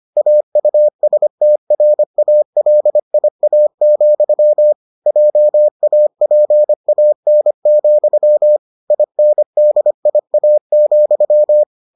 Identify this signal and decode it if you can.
Morse code, heard as a keyed tone, about 25 wpm, 600 Hz. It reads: AUSTRALIA, JAPAN, INDIA,